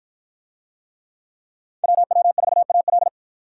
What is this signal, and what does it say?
Morse code, heard as a keyed tone, about 52 wpm, 700 Hz. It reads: WW4AL